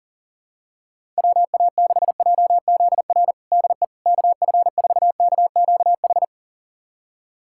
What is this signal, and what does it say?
Morse code, heard as a keyed tone, about 40 wpm, 700 Hz. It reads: WA6JZR DE KF4KQH
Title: USA Callsign DE Callsign - 6 Characters - 40wpm